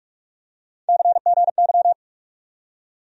Morse code, heard as a keyed tone, about 44 wpm, 700 Hz. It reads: CGY
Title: USA Call Sign Postfixes - 3 Letters 45wpm